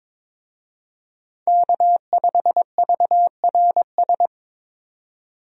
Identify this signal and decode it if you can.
Morse code, heard as a keyed tone, about 22 wpm, 700 Hz. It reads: K5VRS
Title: USA Call Signs - 5 Characters - 22wpm